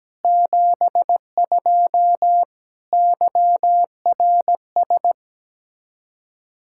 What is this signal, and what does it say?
Morse code, heard as a keyed tone, about 17 wpm, 700 Hz. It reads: 72 YRS
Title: Practice QSO - Exchange 3 - More Info - 17wpm